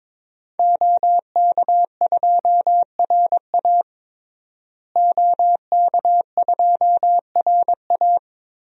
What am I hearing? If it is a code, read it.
Morse code, heard as a keyed tone, about 22 wpm, 700 Hz. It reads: OK2RA OK2RA
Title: DXCC-100 Call Signs - 5 Characters - Repeated 2x - 22wpm